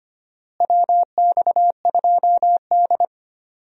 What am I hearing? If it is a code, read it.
Morse code, heard as a keyed tone, about 25 wpm, 700 Hz. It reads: WX2D
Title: USA Call Signs - 4 Characters - Repeated 2x - 25wpm